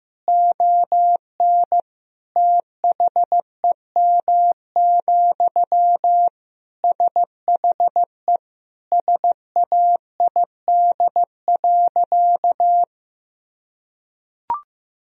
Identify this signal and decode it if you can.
Morse code, heard as a keyed tone, about 15 wpm, 700 Hz. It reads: ON THEM, SHE SAID.
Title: Sentences from Top 1000 Words - Review Part 2 - 15wpm